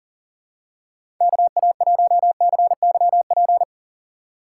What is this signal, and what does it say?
Morse code, heard as a keyed tone, about 40 wpm, 700 Hz. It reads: KA1CYP
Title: USA Call Signs - 6 Characters - 4x Character Spacing - 40wpm